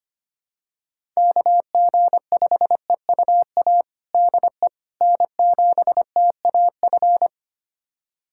Morse code, heard as a keyed tone, about 25 wpm, 700 Hz. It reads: KG5EUA DE N7TAF
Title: USA Callsign DE Callsign 25wpm